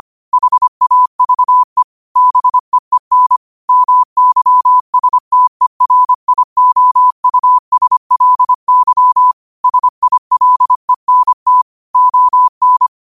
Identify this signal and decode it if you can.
Morse code, heard as a keyed tone, about 25 wpm, 1000 Hz. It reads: HAVE BEEN MYSTERIOUSLY SILENT ON
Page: New quotes every day in morse code at 25 Words per minute.